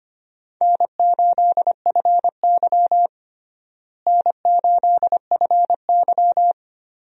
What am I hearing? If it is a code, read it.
Morse code, heard as a keyed tone, about 25 wpm, 700 Hz. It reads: N8FY N8FY
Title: USA Call Signs - 4 Characters - Repeated 3x - 25wpm